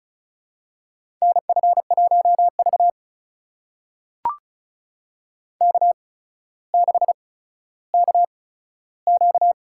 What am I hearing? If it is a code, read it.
Morse code, heard as a keyed tone, about 36 wpm, spaced as for 15 wpm, 700 Hz. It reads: NF1V K6KQ
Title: USA Call Signs - 4 Characters - 8x Character Spacing - 35wpm